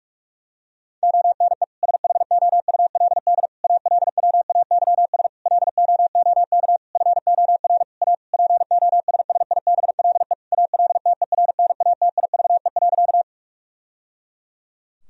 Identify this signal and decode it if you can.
Morse code, heard as a keyed tone, about 46 wpm, 700 Hz. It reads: ONE SHOULD ALWAYS LOOK FOR A POSSIBLE ALTERNATIVE.
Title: Sentences from Top 1000 Words - Review Part 4 - 45wpm